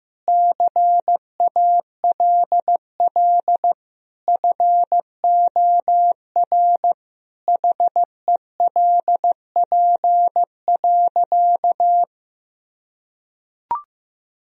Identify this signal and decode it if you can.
Morse code, heard as a keyed tone, about 15 wpm, 700 Hz. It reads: CALL FOR HELP.
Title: Sentences from Top 1000 Words - Review Part 4 - 15wpm